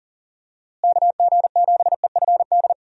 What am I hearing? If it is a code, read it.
Morse code, heard as a keyed tone, about 40 wpm, 700 Hz. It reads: KG7EFD
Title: USA Call Signs - 6 Characters - Repeated 3x - 40wpm